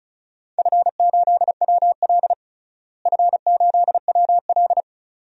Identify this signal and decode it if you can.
Morse code, heard as a keyed tone, about 36 wpm, 700 Hz. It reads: F8WL F8WL
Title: DXCC-100 Call Signs - 4 Characters - Repeated 2x - 35wpm